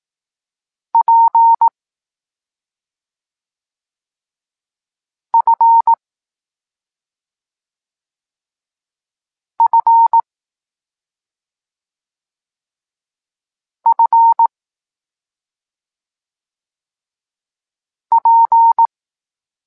Morse code, heard as a keyed tone, about 18 wpm, 920 Hz. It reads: P F F F P
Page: Tekst 25 – Nauka liter P F